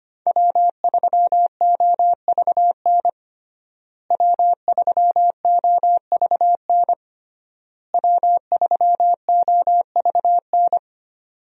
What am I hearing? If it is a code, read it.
Morse code, heard as a keyed tone, about 25 wpm, 700 Hz. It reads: W3OVN W3OVN W3OVN
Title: USA Call Signs - 5 Characters - Repeated 3x - 25wpm